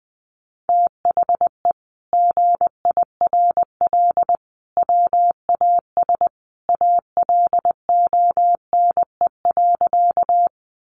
Morse code, heard as a keyed tone, about 20 wpm, 700 Hz. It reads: THE GIRL WAS ALONE.
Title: Sentences from Top 1000 Words 20wpm